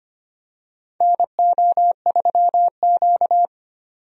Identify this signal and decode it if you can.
Morse code, heard as a keyed tone, about 25 wpm, 700 Hz. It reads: NO3Q
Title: USA Call Signs - 4 Characters - 25wpm